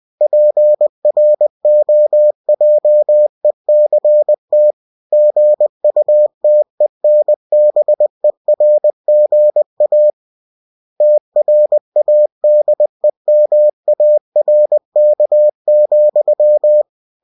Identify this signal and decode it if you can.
Morse code, heard as a keyed tone, about 20 wpm, 600 Hz. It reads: PROJECT GUTENBERGA TRADEMARK,
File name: cw-books / winnie-the-pooh / winnie-20wpm-page-0117